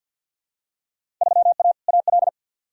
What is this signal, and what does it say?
Morse code, heard as a keyed tone, about 51 wpm, 700 Hz. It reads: 3A AL
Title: Field Day Exchanges - 50wpm